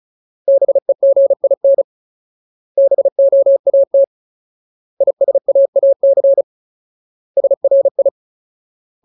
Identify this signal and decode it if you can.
Morse code, heard as a keyed tone, about 34 wpm, 550 Hz. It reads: BEGIN BOAT ISAAC SRI
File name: Ultimate Word List / 35WPM Ultimate